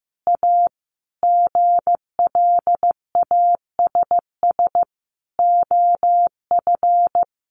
Morse code, heard as a keyed tone, about 14 wpm, 700 Hz. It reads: A GLASS OF
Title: Additional Easy Phrases 15wpm